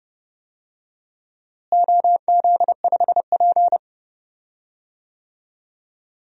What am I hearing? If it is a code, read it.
Morse code, heard as a keyed tone, about 31 wpm, 700 Hz. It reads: OZ5P
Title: DXCC-100 Call Signs - 4 Characters - Repeated 3x - 30wpm